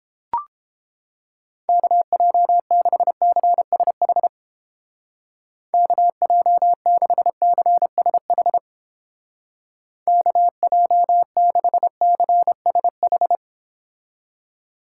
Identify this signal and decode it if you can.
Morse code, heard as a keyed tone, about 29 wpm, 700 Hz. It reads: KJ6CSH KJ6CSH KJ6CSH